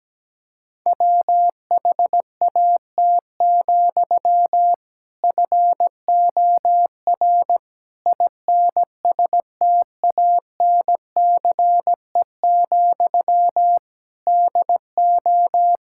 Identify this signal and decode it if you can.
Morse code, heard as a keyed tone, about 17 wpm, 700 Hz. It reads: WHAT, FOR INSTANCE, DO
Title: Sentences from Top 1000 Words - Review Part 5 - 17wpm